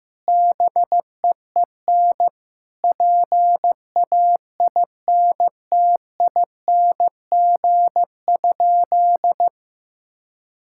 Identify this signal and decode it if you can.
Morse code, heard as a keyed tone, about 15 wpm, 700 Hz. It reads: BEEN PAINTING?
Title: Sentences from Top 1000 Words - Review Part 6 - 15wpm